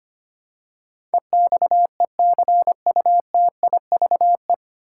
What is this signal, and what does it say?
Morse code, heard as a keyed tone, about 25 wpm, 700 Hz. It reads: EXECUTIVE
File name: Nine Letter Words - 25wpm